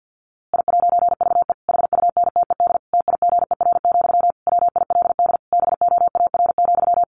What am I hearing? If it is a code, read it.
Morse code, heard as a keyed tone, about 50 wpm, 700 Hz. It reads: I'VE HUNTED TIGER, WILD BOAR,
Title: A Sound Of Thunder 50wpm